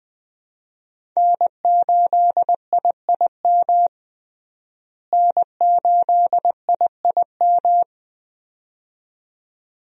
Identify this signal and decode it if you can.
Morse code, heard as a keyed tone, about 21 wpm, 700 Hz. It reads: N8IIM N8IIM
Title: USA Call Signs - 5 Characters - Repeated 2x - 20wpm